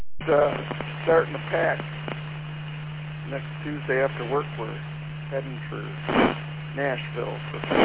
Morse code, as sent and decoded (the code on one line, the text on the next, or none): none